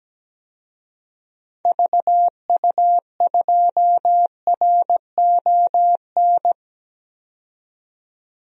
Morse code, heard as a keyed tone, about 17 wpm, 700 Hz. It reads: VU2RON
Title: DXCC-100 Call Signs - 6 Characters - 17wpm